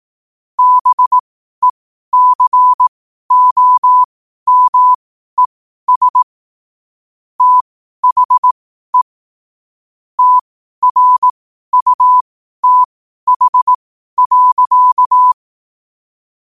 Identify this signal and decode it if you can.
Morse code, heard as a keyed tone, about 18 wpm, spaced as for 12 wpm, 1000 Hz. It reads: BECOMES THE TRUTH.
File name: qotdfiles / 12wpm